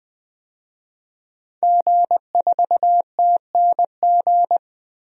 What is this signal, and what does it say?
Morse code, heard as a keyed tone, about 20 wpm, 700 Hz. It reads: G4TNG
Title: DXCC-100 Call Signs - 5 Characters - Repeated 2x - 20wpm